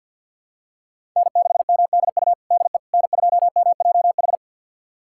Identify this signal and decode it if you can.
Morse code, heard as a keyed tone, about 52 wpm, 700 Hz. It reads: N6MDU DE N2MJH
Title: USA Callsign DE Callsign - 5 Characters - 50wpm